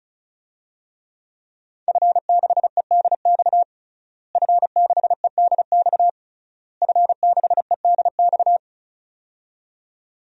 Morse code, heard as a keyed tone, about 36 wpm, 700 Hz. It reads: F6EDX F6EDX F6EDX
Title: DXCC-100 Call Signs - 5 Characters - Repeated 3x - 35wpm